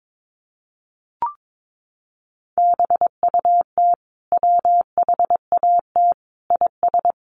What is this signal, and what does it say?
Morse code, heard as a keyed tone, about 22 wpm, 700 Hz. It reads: BUT WHAT IS
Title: Sentences from Top 100 Words 22wpm